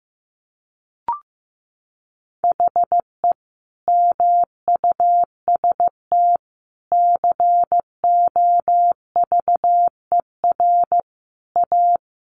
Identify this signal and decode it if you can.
Morse code, heard as a keyed tone, about 15 wpm, 700 Hz. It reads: HE MUST COVER A
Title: Sentences from Top 500 Words 15wpm